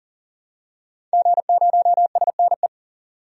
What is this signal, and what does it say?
Morse code, heard as a keyed tone, about 40 wpm, 700 Hz. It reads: G0SNE
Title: DXCC-100 Call Signs - 5 Characters - 40wpm